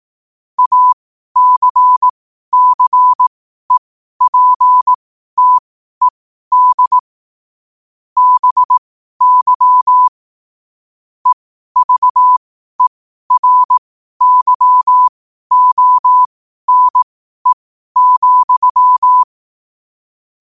Morse code, heard as a keyed tone, about 18 wpm, spaced as for 12 wpm, 1000 Hz. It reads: ACCEPTED BY EVERYONE,
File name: qotdfiles / 12wpm